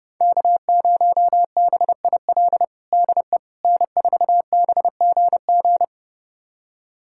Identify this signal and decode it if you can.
Morse code, heard as a keyed tone, about 30 wpm, 700 Hz. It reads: K0BIL DE N4BGG
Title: USA Callsign DE Callsign - 5 Characters - 30wpm